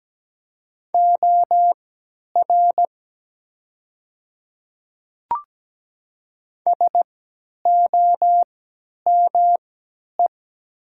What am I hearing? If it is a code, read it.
Morse code, heard as a keyed tone, about 17 wpm, spaced as for 10 wpm, 700 Hz. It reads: OR SOME
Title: 3x Character Spacing - Top 100 Words - 17wpm